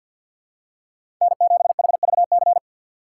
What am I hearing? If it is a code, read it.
Morse code, heard as a keyed tone, about 53 wpm, 700 Hz. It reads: N7HVC
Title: USA Call Signs - 5 Characters - 3x Character Spacing - 50wpm